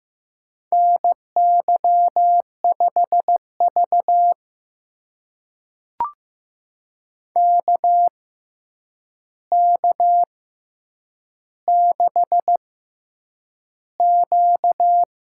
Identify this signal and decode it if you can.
Morse code, heard as a keyed tone, about 15 wpm, spaced as for 8 wpm, 700 Hz. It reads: NY5V KK6Q
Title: USA Call Signs - 4 Characters - 6x Character Spacing - 15wpm